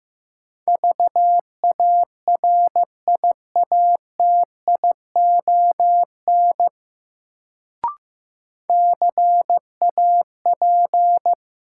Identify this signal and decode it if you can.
Morse code, heard as a keyed tone, about 15 wpm, 700 Hz. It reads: VARIATION CAP
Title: Sentences from Top 1600 Words - New Word Drill - 15wpm